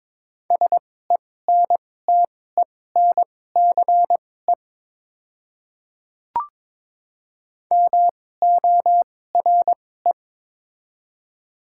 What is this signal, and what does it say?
Morse code, heard as a keyed tone, about 22 wpm, spaced as for 16 wpm, 690 Hz. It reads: SENTENCE MORE
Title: 2x Character Spacing - Top 200 Words - 22wpm